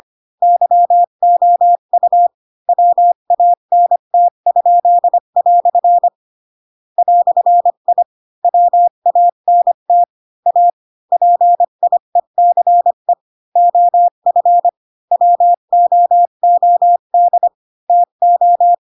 Morse code, heard as a keyed tone, about 25 wpm, 700 Hz. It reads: YOU WANT?" "I WANT A PIECE OF WOOD TO